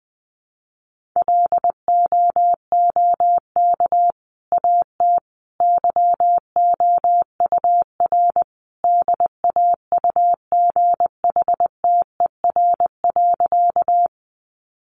Morse code, heard as a keyed tone, about 20 wpm, 700 Hz. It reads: LOOK AT YOUR DAUGHTER.
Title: Sentences from Top 1500 Words 20wpm